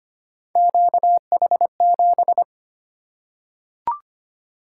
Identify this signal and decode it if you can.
Morse code, heard as a keyed tone, about 25 wpm, 700 Hz. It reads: QH7
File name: Three Letter-Number Combinations - 25wpm